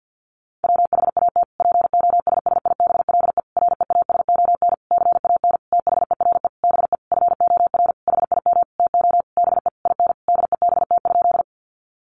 Morse code, heard as a keyed tone, about 50 wpm, 700 Hz. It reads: WHAT POSSIBLE REASON CAN THERE BE FOR HIM TO BE IN DEBT?
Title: Sentences from Top 1300 Words 50wpm